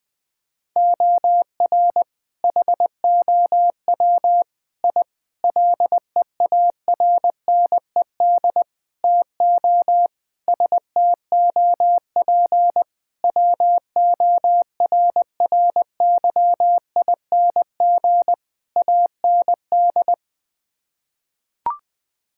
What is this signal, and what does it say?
Morse code, heard as a keyed tone, about 20 wpm, 700 Hz. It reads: OR HOW I LEARNED TO STOP WORRYING AND
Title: Sets of 8 Words - 20wpm